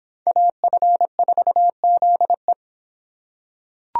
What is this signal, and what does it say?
Morse code, heard as a keyed tone, about 25 wpm, 700 Hz. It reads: AF4ZE